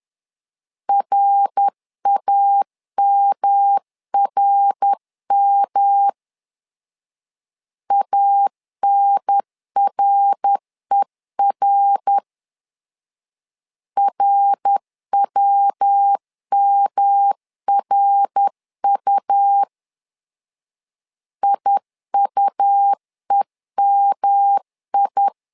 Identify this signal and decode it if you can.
Morse code, heard as a keyed tone, about 10 wpm, 800 Hz. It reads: RAMRM ANRER RWMRU IUEMI
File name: LEZIONI / 3 MWRO / 3 MWR 10wpm 1952